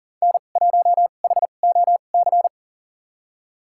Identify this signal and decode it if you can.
Morse code, heard as a keyed tone, about 40 wpm, spaced as for 29 wpm, 700 Hz. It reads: N1HOC
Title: USA Call Signs - 5 Characters - 2x Character Spacing - 40wpm